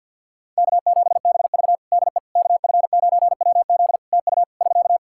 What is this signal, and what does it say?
Morse code, heard as a keyed tone, about 52 wpm, 700 Hz. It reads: K7BV DE KF9WZ TU <SK>
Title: K1USN SST QSO Practice 50wpm